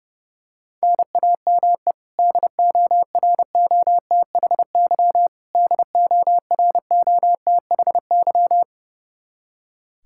Morse code, heard as a keyed tone, about 32 wpm, 700 Hz. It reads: NAME DOROTHY DOROTHY
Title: Practice QSO - Exchange 1 - Beginning - 30wpm